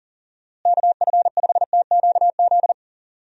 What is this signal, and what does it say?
Morse code, heard as a keyed tone, about 40 wpm, 700 Hz. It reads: KF5TQZ
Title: USA Call Signs - 6 Characters - Repeated 2x - 40wpm